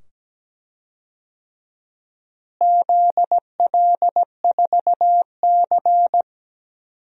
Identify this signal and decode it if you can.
Morse code, heard as a keyed tone, about 17 wpm, 700 Hz. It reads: ZL4C
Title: DXCC-100 Call Signs - 4 Characters - Repeated 2x - 17wpm